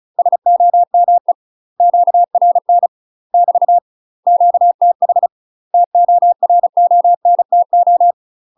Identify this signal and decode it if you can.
Morse code, heard as a keyed tone, about 34 wpm, 700 Hz. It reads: SOME QRN <BT> QTH TORONTO
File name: UK MASTER QSOs TRAINING / 35WPM / UK MASTER QSOs 0039